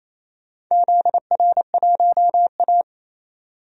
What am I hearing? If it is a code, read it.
Morse code, heard as a keyed tone, about 27 wpm, 700 Hz. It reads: ZR1A
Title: DXCC-100 Call Signs - 4 Characters - 28wpm